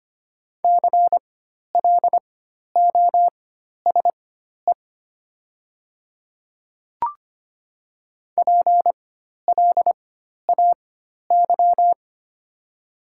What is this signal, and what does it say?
Morse code, heard as a keyed tone, about 25 wpm, spaced as for 12 wpm, 700 Hz. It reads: CLOSE PLAY
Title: 4x Character Spacing - Top 300 Words - 25wpm